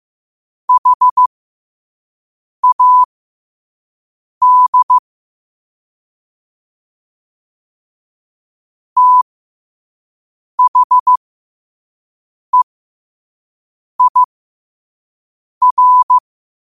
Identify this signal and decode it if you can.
Morse code, heard as a keyed tone, about 15 wpm, spaced as for 5 wpm, 1000 Hz. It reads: HAD THEIR
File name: qotdfiles / 5wpm